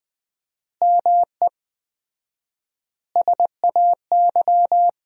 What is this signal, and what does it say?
Morse code, heard as a keyed tone, about 20 wpm, 700 Hz. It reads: ME SAY
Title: 4x Word-Spacing - Sentences from Top 300 Words - 20wpm